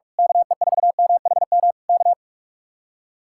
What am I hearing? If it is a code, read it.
Morse code, heard as a keyed tone, about 46 wpm, 700 Hz. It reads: KE4MHM K
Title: Speed-Racing Common QSO Phrases - 45wpm